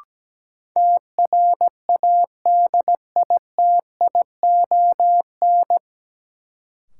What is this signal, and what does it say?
Morse code, heard as a keyed tone, about 17 wpm, 700 Hz. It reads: TRADITION
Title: Sentences from Top 1500 Words - New Word Drill - 17wpm